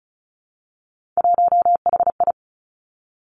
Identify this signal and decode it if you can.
Morse code, heard as a keyed tone, about 36 wpm, 700 Hz. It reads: 1HI
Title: Three Challenging Letter-Number Combinations 35wpm